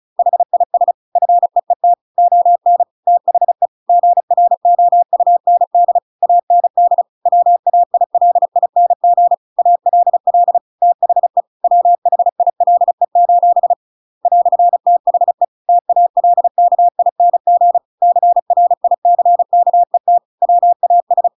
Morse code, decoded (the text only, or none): HIS FEET ON THE GROUND AND WAILING ALL THE WHILE: "THE TALKING CRICKET WAS